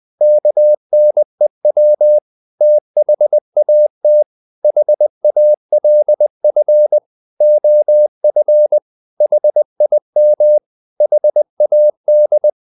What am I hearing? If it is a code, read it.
Morse code, heard as a keyed tone, about 20 wpm, 600 Hz. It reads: KNEW THAT HALF OF HIM HAD
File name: the old man and the sea part 2 20wpm